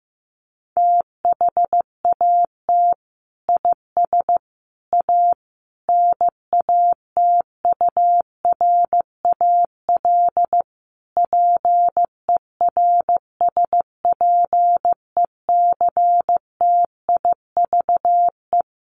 Morse code, heard as a keyed tone, about 15 wpm, 700 Hz. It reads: THAT IS A NATURAL PERSPECTIVE
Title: Sentences from Top 1200 Words 15wpm